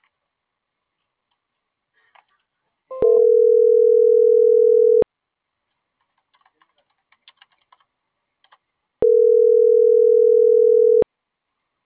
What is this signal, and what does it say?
Ringback tone (North America), 440 + 480 Hz.